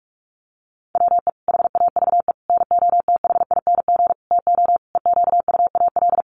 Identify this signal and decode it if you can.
Morse code, heard as a keyed tone, about 44 wpm, 700 Hz. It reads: WE HAVE NOTHING TO EQUAL
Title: Sentences from Top 1400 Words 45wpm